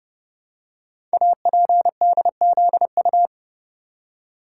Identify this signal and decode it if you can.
Morse code, heard as a keyed tone, about 30 wpm, 690 Hz. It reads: APDZU
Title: Code Groups - 5 Letters - 30wpm